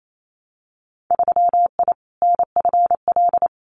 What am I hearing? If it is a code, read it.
Morse code, heard as a keyed tone, about 28 wpm, 700 Hz. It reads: 3I NFL
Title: Winter Field Day Exchanges 28wpm